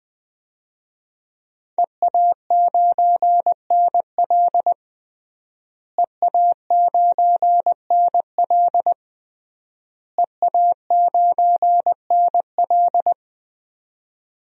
Morse code, heard as a keyed tone, about 20 wpm, 700 Hz. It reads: EA9NL EA9NL EA9NL
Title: DXCC-100 Call Signs - 5 Characters - Repeated 3x - 20wpm